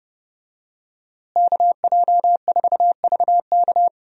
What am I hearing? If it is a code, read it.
Morse code, heard as a keyed tone, about 30 wpm, 700 Hz. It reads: KJ4VK
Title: USA Call Signs - 5 Characters - Repeated 3x - 30wpm